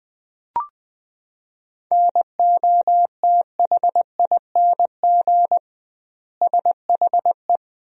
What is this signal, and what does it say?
Morse code, heard as a keyed tone, about 20 wpm, 700 Hz. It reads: NOTHING SHE
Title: 2x Word-Spacing - Sentences from Top 500 Words - 20wpm